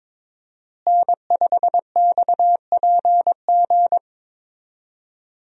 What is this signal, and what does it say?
Morse code, heard as a keyed tone, about 22 wpm, 700 Hz. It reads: N5XPG
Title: USA Call Signs - 5 Characters - Repeated 3x - 22wpm